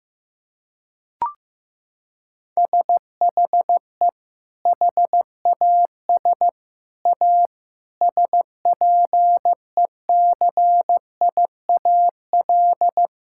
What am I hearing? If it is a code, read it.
Morse code, heard as a keyed tone, about 15 wpm, 700 Hz. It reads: SHE HAS A SPECIAL
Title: Sentences from Top 1000 Words - Review Part 5 - 15wpm